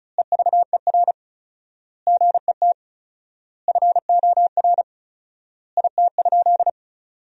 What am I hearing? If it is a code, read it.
Morse code, heard as a keyed tone, about 36 wpm, 700 Hz. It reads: EVER GET FOR IT?
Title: 4x Word-Spacing - Sentences from Top 400 Words - 35wpm